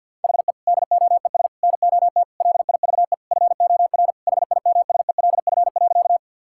Morse code, heard as a keyed tone, about 50 wpm, 700 Hz. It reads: HE DOES NOT LIVE FOR HIMSELF.
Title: Sentences from Top 1600 Words 50wpm